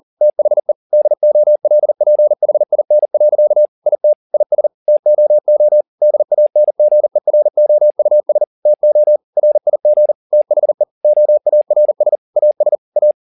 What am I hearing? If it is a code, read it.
Morse code, heard as a keyed tone, about 40 wpm, 600 Hz. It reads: THE DOLPHIN. IT IS TOO DANGEROUS TO RIG THE OARS AS A
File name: the old man and the sea 2 40wpm